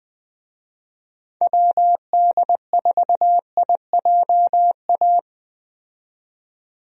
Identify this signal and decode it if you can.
Morse code, heard as a keyed tone, about 20 wpm, 700 Hz. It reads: WD4IJA
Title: USA Call Signs - 6 Characters - 20wpm